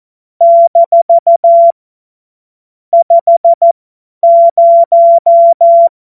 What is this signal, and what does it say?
Morse code, heard as a keyed tone, about 14 wpm, spaced as for 10 wpm, 680 Hz. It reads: - 50
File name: Morsekurs / Witze / Witz 23. August 2022 50 F70